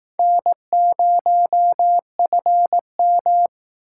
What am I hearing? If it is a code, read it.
Morse code, heard as a keyed tone, about 19 wpm, 700 Hz. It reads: N0FM
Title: Speed Racing - USA 1x2 Call Signs 15wpm